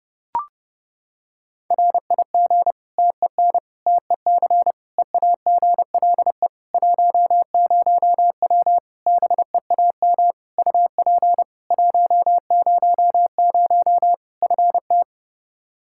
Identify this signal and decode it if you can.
Morse code, heard as a keyed tone, about 30 wpm, 700 Hz. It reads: RIG TEN TEC EAGLE 10W BEAM UP 100 FT
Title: Practice QSO - Exchange 2 - Info - 30wpm